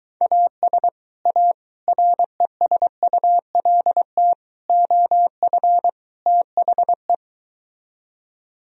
Morse code, heard as a keyed tone, about 22 wpm, 700 Hz. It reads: AS A RESULT OF THE